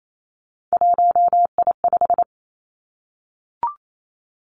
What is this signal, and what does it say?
Morse code, heard as a keyed tone, about 28 wpm, 700 Hz. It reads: 1I5
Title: Three Challenging Letter-Number Combinations 28wpm